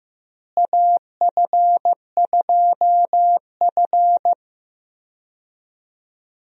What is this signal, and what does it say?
Morse code, heard as a keyed tone, about 15 wpm, 700 Hz. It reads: AF2F
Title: USA Call Signs - 4 Characters - Repeated 3x - 15wpm